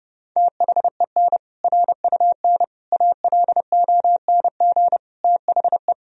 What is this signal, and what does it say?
Morse code, heard as a keyed tone, about 31 wpm, 700 Hz. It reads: THEN RUN ALONG THE
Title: Sentences from Top 1000 Words - Review Part 2 - 30wpm